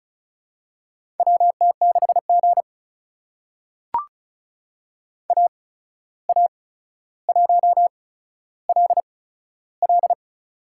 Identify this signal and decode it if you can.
Morse code, heard as a keyed tone, about 34 wpm, spaced as for 14 wpm, 700 Hz. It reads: WT6G AA1LL
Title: USA Call Signs - 8x Character Spacing - 35wpm